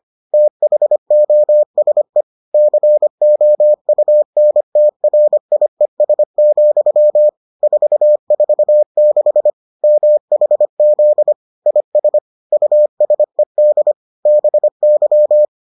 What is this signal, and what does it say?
Morse code, heard as a keyed tone, about 25 wpm, 600 Hz. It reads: THOSE COUNTRIES, 446 MHZ IS USED BY